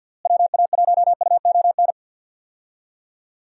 Morse code, heard as a keyed tone, about 52 wpm, 700 Hz. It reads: WA1UON
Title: USA Call Signs - 6 Characters - Repeated 2x - 50wpm